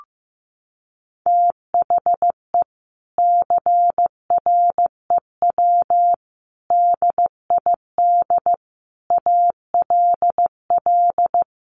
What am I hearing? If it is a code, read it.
Morse code, heard as a keyed tone, about 15 wpm, 700 Hz. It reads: THE CREW DID ALL
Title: Sentences from Top 1500 Words 15wpm